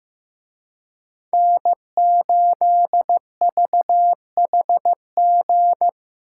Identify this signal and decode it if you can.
Morse code, heard as a keyed tone, about 18 wpm, 700 Hz. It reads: N8VHG
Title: Speed Racing - USA 1x3 Call Signs 15wpm